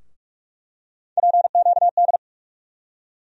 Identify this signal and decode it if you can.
Morse code, heard as a keyed tone, about 45 wpm, 700 Hz. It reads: PQD
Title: USA Call Sign Postfixes - 3 Letters 45wpm